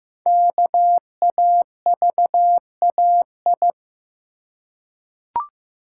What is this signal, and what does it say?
Morse code, heard as a keyed tone, about 15 wpm, 700 Hz. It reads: KAVAI
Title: Code Groups - 5 Letters - 15wpm